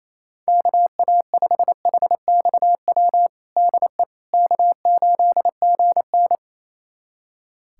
Morse code, heard as a keyed tone, about 28 wpm, 700 Hz. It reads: KA5HXW DE K8GN
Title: USA Callsign DE Callsign 28wpm